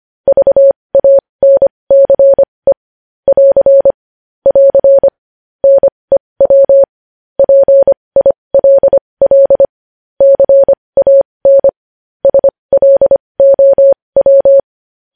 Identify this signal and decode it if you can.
Morse code, heard as a keyed tone, about 25 wpm, spaced as for 20 wpm, 550 Hz. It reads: VANCE <AR> <AR> NEW PILL CAN SLOW
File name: podcast / News headlines-1x-25x20